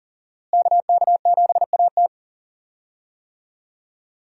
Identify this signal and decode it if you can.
Morse code, heard as a keyed tone, about 37 wpm, 700 Hz. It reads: KK7AT
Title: USA Call Signs - 5 Characters - Repeated 3x - 40wpm